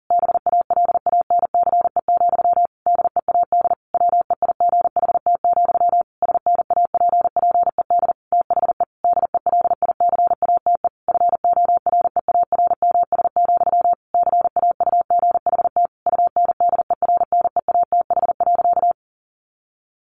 Morse code, heard as a keyed tone, about 40 wpm, 700 Hz. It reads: BALANCE, DEAD WEIGHT, SNAPPED THE DELICATE FOREARMS, CAUGHT UNDERNEATH.
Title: A Sound Of Thunder 40wpm